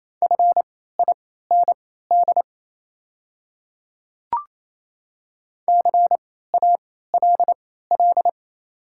Morse code, heard as a keyed tone, about 28 wpm, spaced as for 16 wpm, 690 Hz. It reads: FIND CALL
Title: 3x Character Spacing - Top 100 Words - 28wpm